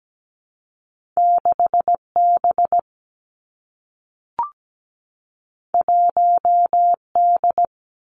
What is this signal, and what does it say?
Morse code, heard as a keyed tone, about 17 wpm, 700 Hz. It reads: 6B 1D
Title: Two Challenging Letter-Number Combinations 17wpm